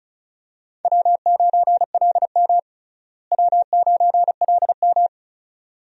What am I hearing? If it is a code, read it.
Morse code, heard as a keyed tone, about 34 wpm, 700 Hz. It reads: W9LM W9LM
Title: USA Call Signs - 4 Characters - Repeated 2x - 35wpm